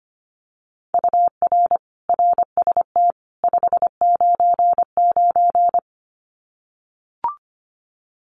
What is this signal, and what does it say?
Morse code, heard as a keyed tone, about 25 wpm, 700 Hz. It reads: UR RST 599
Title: Common QSO Phrases 25wpm